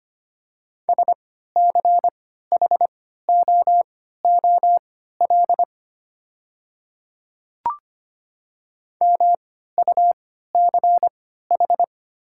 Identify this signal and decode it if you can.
Morse code, heard as a keyed tone, about 25 wpm, spaced as for 14 wpm, 690 Hz. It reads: SCHOOL MUCH
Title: 3x Character Spacing - Top 500 Words - 25wpm